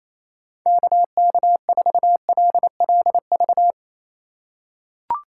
Morse code, heard as a keyed tone, about 28 wpm, 690 Hz. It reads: KK4LLV
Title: USA Call Signs - 6 Characters - 4x Character Spacing - 28wpm